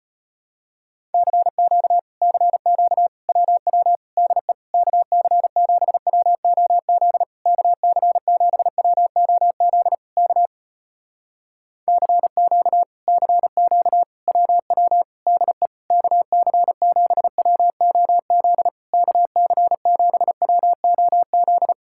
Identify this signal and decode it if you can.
Morse code, heard as a keyed tone, about 34 wpm, 700 Hz. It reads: CQ CQ WW DE KC7WOZ KC7WOZ K CQ CQ WW DE KC7WOZ KC7WOZ